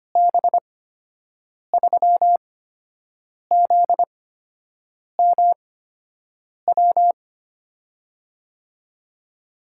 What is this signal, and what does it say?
Morse code, heard as a keyed tone, about 25 wpm, spaced as for 7 wpm, 700 Hz. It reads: B3ZMW
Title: USA Call Signs - 6 Characters - 8x Character Spacing - 25wpm